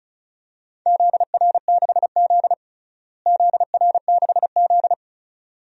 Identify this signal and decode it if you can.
Morse code, heard as a keyed tone, about 36 wpm, 700 Hz. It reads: ZR6Z ZR6Z
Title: DXCC-100 Call Signs - 4 Characters - Repeated 2x - 35wpm